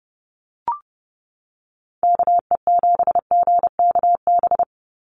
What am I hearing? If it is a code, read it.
Morse code, heard as a keyed tone, about 30 wpm, 700 Hz. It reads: KE7GKB
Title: USA Call Signs 30wpm